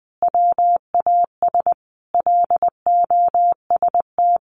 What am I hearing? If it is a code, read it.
Morse code, heard as a keyed tone, about 19 wpm, 700 Hz. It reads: WAS LOST
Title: Sentences from Top 1400 Words 20wpm